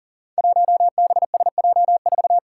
Morse code, heard as a keyed tone, about 40 wpm, 700 Hz. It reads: 1BSJ4
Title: Five Challenging Letter-Number Combinations 40wpm